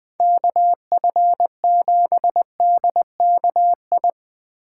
Morse code, heard as a keyed tone, about 23 wpm, 700 Hz. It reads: KF7DKI
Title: Speed Racing - USA 2x3 Call Signs 20wpm